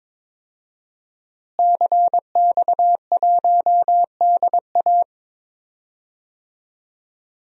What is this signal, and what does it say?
Morse code, heard as a keyed tone, about 23 wpm, 700 Hz. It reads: CX1DA
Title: DXCC-100 Call Signs - 5 Characters - Repeated 3x - 22wpm